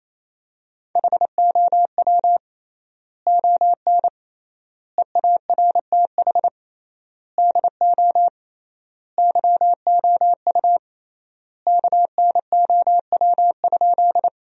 Morse code, heard as a keyed tone, about 27 wpm, 700 Hz. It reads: HOW ON EARTH DO YOU KNOW?
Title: 3x Word-Spacing - Sentences from Top 200 Words - 28wpm